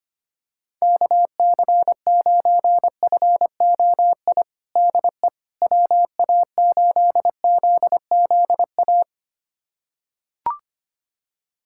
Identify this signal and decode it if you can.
Morse code, heard as a keyed tone, about 25 wpm, 700 Hz. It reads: KC9FOI DE WA8ZZA
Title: USA Callsign DE Callsign - 6 Characters - 25wpm